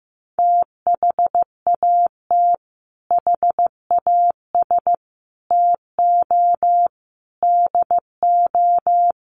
Morse code, heard as a keyed tone, about 15 wpm, 700 Hz. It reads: THAT HAS TO DO
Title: Sets of 5 Words Encore2 15wpm